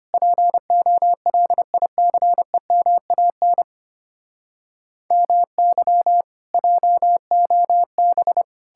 Morse code, heard as a keyed tone, about 33 wpm, 700 Hz. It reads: POLICEMAN MY JOB